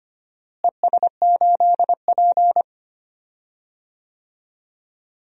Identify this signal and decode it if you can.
Morse code, heard as a keyed tone, about 25 wpm, 700 Hz. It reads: ES8P
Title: DXCC-100 Call Signs - 4 Characters - Repeated 3x - 25wpm